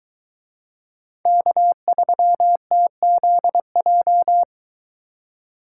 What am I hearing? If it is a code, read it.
Morse code, heard as a keyed tone, about 23 wpm, 700 Hz. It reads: K3TZJ